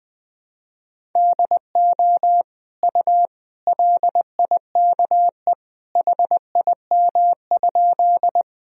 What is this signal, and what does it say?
Morse code, heard as a keyed tone, about 20 wpm, 700 Hz. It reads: DO U LIKE HIM?
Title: Sentences from Top 100 Words and Abbreviations 20wpm